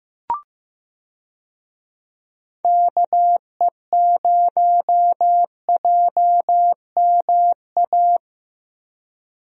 Morse code, heard as a keyed tone, about 15 wpm, 700 Hz. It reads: KE0JMA
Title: USA Call Signs - 6 Characters - 15wpm